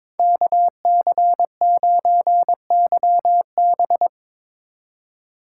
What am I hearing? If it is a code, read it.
Morse code, heard as a keyed tone, about 25 wpm, 700 Hz. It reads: KC9YB
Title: Speed Racing - USA 2x2 Call Signs 22wpm